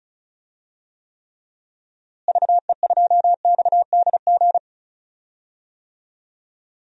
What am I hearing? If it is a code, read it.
Morse code, heard as a keyed tone, about 34 wpm, 700 Hz. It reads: VE2XDG
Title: DXCC-100 Call Signs - 6 Characters - Repeated 3x - 35wpm